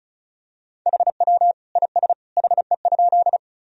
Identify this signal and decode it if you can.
Morse code, heard as a keyed tone, about 34 wpm, 700 Hz. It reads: HW IS HE?
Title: Sentences from Top 100 Words and Abbreviations 35wpm